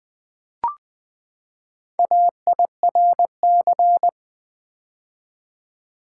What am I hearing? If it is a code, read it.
Morse code, heard as a keyed tone, about 20 wpm, 700 Hz. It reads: AIRC
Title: Build A Word - 4 Letters At A Time - 8-Letter Words - 20wpm